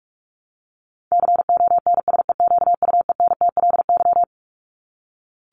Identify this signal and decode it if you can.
Morse code, heard as a keyed tone, about 44 wpm, 700 Hz. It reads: CONSEQUENTLY
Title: LY Suffix - Top 100 Words with Base Words 45wpm